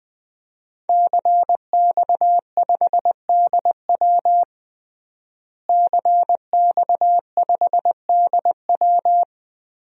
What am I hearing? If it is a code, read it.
Morse code, heard as a keyed tone, about 21 wpm, 700 Hz. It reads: CX5DW CX5DW
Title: DXCC-100 Call Signs - 5 Characters - Repeated 3x - 20wpm